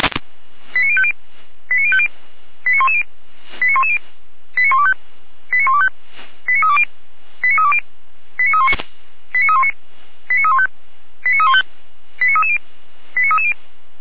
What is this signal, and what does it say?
Five-tone selective call (ZVEI), heard as quick runs of five tones, tones of about 70 ms. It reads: 89050 89050 89110 89110 89125 89125 89320 89320 89319 89319 89315 89315 89330 89330